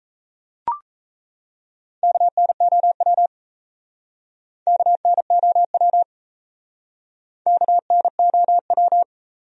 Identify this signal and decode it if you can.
Morse code, heard as a keyed tone, about 38 wpm, 700 Hz. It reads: KNOW KNOW KNOW